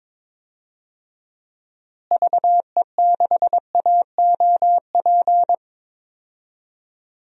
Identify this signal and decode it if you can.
Morse code, heard as a keyed tone, about 22 wpm, 700 Hz. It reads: VE6AOP
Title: DXCC-100 Call Signs - 6 Characters - Repeated 2x - 22wpm